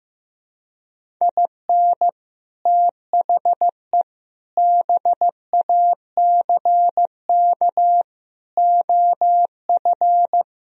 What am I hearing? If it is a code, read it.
Morse code, heard as a keyed tone, about 15 wpm, 700 Hz. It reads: IN THE BACK OF
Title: Speed-Racing - Sets of 5 Words 15wpm